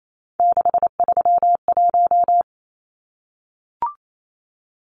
Morse code, heard as a keyed tone, about 28 wpm, 700 Hz. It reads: 631
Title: Numbers - 3-Digit 28wpm